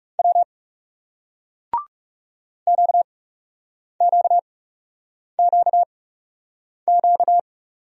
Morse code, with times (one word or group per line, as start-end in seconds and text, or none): 0.2-0.4 W
2.7-7.4 QQQQ